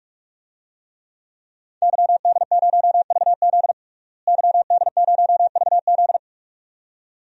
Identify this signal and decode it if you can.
Morse code, heard as a keyed tone, about 46 wpm, 700 Hz. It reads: YD0VZ YD0VZ
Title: DXCC-100 Call Signs - 5 Characters - Repeated 2x - 45wpm